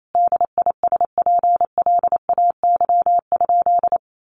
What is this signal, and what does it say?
Morse code, heard as a keyed tone, about 28 wpm, 700 Hz. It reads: DISPLAY?
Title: Sentences from Top 1200 Words 28wpm